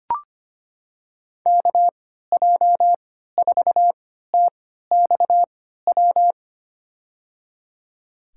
Morse code, heard as a keyed tone, about 25 wpm, spaced as for 14 wpm, 700 Hz. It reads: KJ4TXW
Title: USA Call Signs - 6 Characters - 3x Character Spacing - 25wpm